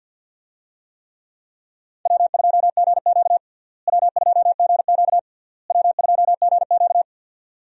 Morse code, heard as a keyed tone, about 51 wpm, 700 Hz. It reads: W2GQ W2GQ W2GQ
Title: USA Call Signs - 4 Characters - Repeated 3x - 50wpm